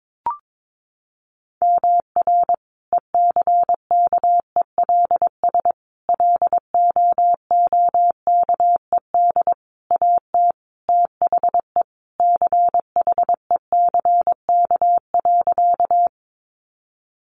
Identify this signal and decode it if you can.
Morse code, heard as a keyed tone, about 22 wpm, 700 Hz. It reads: MR ECKELS LOOKED AT THE CHECK.
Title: A Sound Of Thunder 22wpm